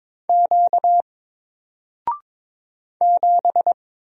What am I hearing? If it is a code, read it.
Morse code, heard as a keyed tone, about 22 wpm, 700 Hz. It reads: Q 7
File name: Single-Letter-Number-22wpm